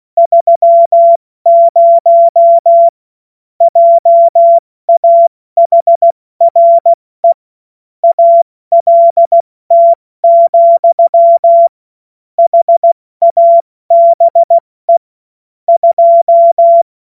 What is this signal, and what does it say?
Morse code, heard as a keyed tone, about 16 wpm, spaced as for 14 wpm, 680 Hz. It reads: 30 JAHRE ALT, HABE 2
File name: Morsekurs / Witze / Witz 03. August 2021, 70 F80